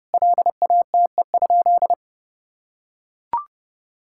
Morse code, heard as a keyed tone, about 30 wpm, 700 Hz. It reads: LATE?
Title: Sentences from Top 1000 Words - Review Part 5 - 30wpm